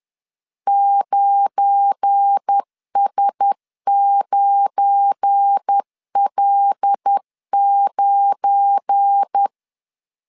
Morse code, heard as a keyed tone, about 10 wpm, 800 Hz. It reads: 9S9L9
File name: LEZIONI / 7 GF9Z / 7 GF9 10wpm 2747